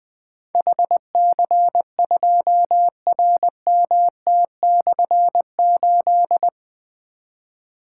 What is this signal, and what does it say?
Morse code, heard as a keyed tone, about 20 wpm, 700 Hz. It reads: HC2RMT/8
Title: Difficult Call Signs 20wpm